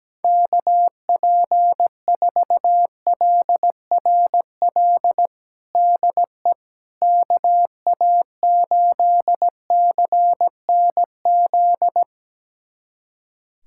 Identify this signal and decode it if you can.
Morse code, heard as a keyed tone, about 17 wpm, 700 Hz. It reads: KP4LRL DE KA8CNZ
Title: USA Callsign DE Callsign 17wpm